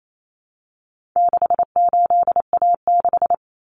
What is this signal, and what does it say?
Morse code, heard as a keyed tone, about 28 wpm, 700 Hz. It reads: 68A6
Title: Common Cut Numbers - 4-Digit 28wpm